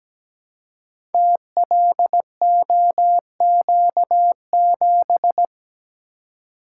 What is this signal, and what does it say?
Morse code, heard as a keyed tone, about 17 wpm, 700 Hz. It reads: TLOQ7
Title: Code Groups - 5 Letter:Numbers - 17wpm